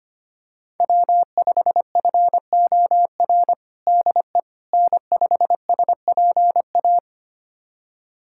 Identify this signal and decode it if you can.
Morse code, heard as a keyed tone, about 25 wpm, 700 Hz. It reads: W5FOR DE N5SPA
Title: USA Callsign DE Callsign - 5 Characters - 25wpm